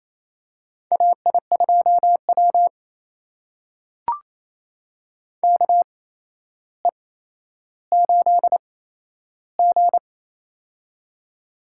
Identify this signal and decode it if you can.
Morse code, heard as a keyed tone, about 28 wpm, spaced as for 12 wpm, 700 Hz. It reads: AI2W KE8G
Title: USA Call Signs - 4 Characters - 8x Character Spacing - 28wpm